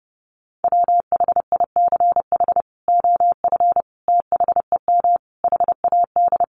Morse code, heard as a keyed tone, about 30 wpm, 700 Hz. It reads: WHICH OF THEM HAD
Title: Sentences from Top 1100 Words 30wpm